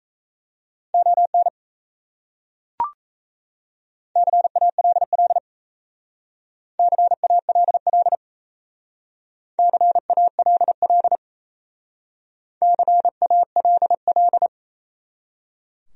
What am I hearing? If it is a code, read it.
Morse code, heard as a keyed tone, about 36 wpm, 700 Hz. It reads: ON CALL CALL CALL CALL